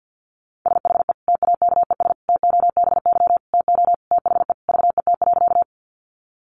Morse code, heard as a keyed tone, about 50 wpm, 700 Hz. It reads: SHE TAKES TOBY TO THE VET.
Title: Easy Phrases 50wpm